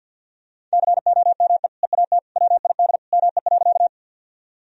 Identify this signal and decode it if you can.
Morse code, heard as a keyed tone, about 46 wpm, 700 Hz. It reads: COME EAT WID ME.
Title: Sentences from Top 300 Words and Abbreviations 50wpm